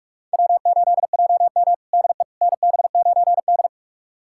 Morse code, heard as a keyed tone, about 46 wpm, 700 Hz. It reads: W8JM DE NB9D
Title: USA Callsign DE Callsign - 4 Characters - 45wpm